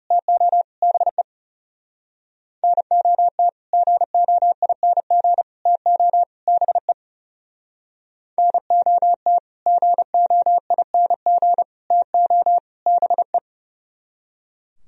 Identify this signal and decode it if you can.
Morse code, heard as a keyed tone, about 36 wpm, 700 Hz. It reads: TO BE NOT GOING TO BE NOT GOING TO BE
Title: Speed-Racing - Sets of 4 Words 30wpm